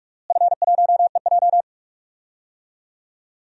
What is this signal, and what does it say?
Morse code, heard as a keyed tone, about 46 wpm, 690 Hz. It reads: F1EJ
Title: DXCC-100 Call Signs - 4 Characters - Repeated 3x - 45wpm